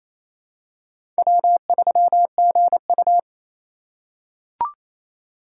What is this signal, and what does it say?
Morse code, heard as a keyed tone, about 28 wpm, 700 Hz. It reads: W3GU
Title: USA Call Signs - 3x Character Spacing - 28wpm